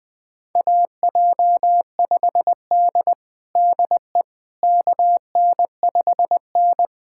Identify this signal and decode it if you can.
Morse code, heard as a keyed tone, about 21 wpm, 700 Hz. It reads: AJ5D DE KN5N
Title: USA Callsign DE Callsign - 4 Characters - 20wpm